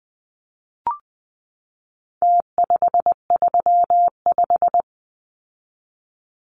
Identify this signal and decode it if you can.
Morse code, heard as a keyed tone, about 20 wpm, 700 Hz. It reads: T535
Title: Numbers - Contest Style - 4-Digit 20wpm